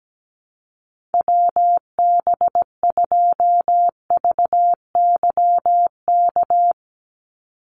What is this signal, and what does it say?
Morse code, heard as a keyed tone, about 17 wpm, 700 Hz. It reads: WB2VYK
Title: Call Signs - 17wpm